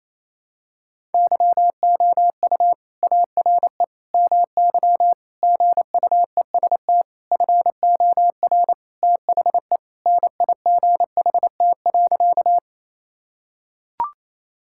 Morse code, heard as a keyed tone, about 28 wpm, 700 Hz. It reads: YOU ARE MY GUEST FOR THE NIGHT.
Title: Sentences from Top 1600 Words 28wpm